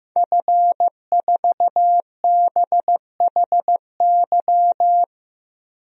Morse code, heard as a keyed tone, about 15 wpm, 700 Hz. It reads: F4BHY
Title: DXCC-100 Call Signs - 5 Characters - Repeated 3x - 15wpm